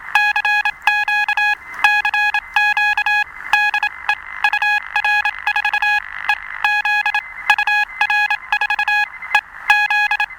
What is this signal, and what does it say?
Morse code, heard as a keyed tone, about 27 wpm, spaced as for 22 wpm, 1650 Hz, with noise. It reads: CQ CQ DE UR4EZ UR4EZ
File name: attachments / 16.05.2017 UR4EZ de UA1CEG p